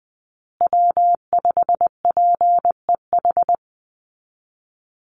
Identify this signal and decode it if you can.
Morse code, heard as a keyed tone, about 20 wpm, 700 Hz. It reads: W5PEH
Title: Call Signs Repeated 3x - 20wpm